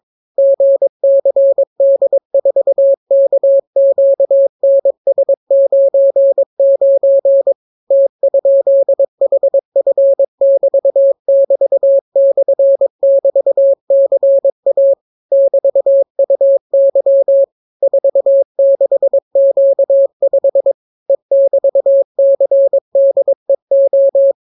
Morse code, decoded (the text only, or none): GCD4KQNS99 T?HF<BT><BT>/<BT>CA <BT>UY 46Q5 E<BT>CDEO